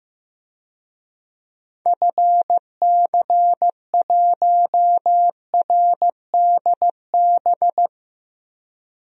Morse code, heard as a keyed tone, about 15 wpm, 700 Hz. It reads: FC1RDB
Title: DXCC-100 Call Signs - 6 Characters - Repeated 3x - 15wpm